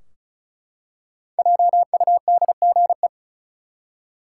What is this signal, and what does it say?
Morse code, heard as a keyed tone, about 34 wpm, 700 Hz. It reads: JUDGE
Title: Five Letter Words - 35wpm